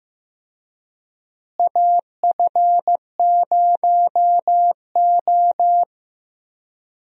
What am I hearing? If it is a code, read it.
Morse code, heard as a keyed tone, about 15 wpm, 700 Hz. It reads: AF0O
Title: USA Call Signs - 4 Characters - Repeated 3x - 15wpm